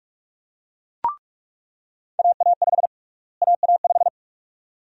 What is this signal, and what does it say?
Morse code, heard as a keyed tone, about 46 wpm, 700 Hz. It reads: AA5 AA5
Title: USA Call Sign Prefixes - Repeated 2x - 45wpm